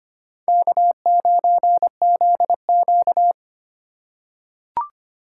Morse code, heard as a keyed tone, about 25 wpm, 700 Hz. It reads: K9ZQ
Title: USA Call Signs - 3x Character Spacing - 25wpm